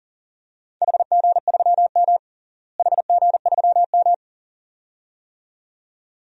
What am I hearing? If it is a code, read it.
Morse code, heard as a keyed tone, about 40 wpm, 690 Hz. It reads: HG3M HG3M
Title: DXCC-100 Call Signs - 4 Characters - Repeated 2x - 40wpm